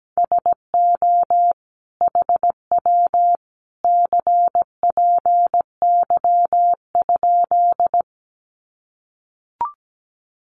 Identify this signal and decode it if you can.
Morse code, heard as a keyed tone, about 17 wpm, 700 Hz. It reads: SO HW CPY?
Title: Common QSO Phrases 17wpm